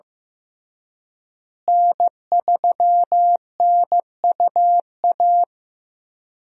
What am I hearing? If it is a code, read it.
Morse code, heard as a keyed tone, about 15 wpm, 700 Hz. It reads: N3NUA
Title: USA Call Signs - 5 Characters - Repeated 3x - 15wpm